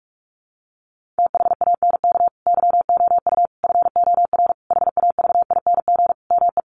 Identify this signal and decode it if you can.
Morse code, heard as a keyed tone, about 44 wpm, 700 Hz. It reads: THANK YOU FOR HAVING ME
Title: Sets of 5 Words Encore 45wpm